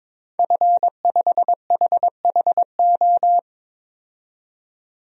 Morse code, heard as a keyed tone, about 22 wpm, 700 Hz. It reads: F5HHO
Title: DXCC-100 Call Signs - 5 Characters - 22wpm